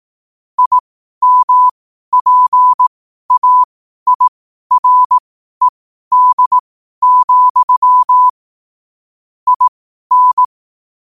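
Morse code, heard as a keyed tone, about 18 wpm, spaced as for 12 wpm, 1000 Hz. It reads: IMPAIRED, IN
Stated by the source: Morse practice sent at 12 wpm.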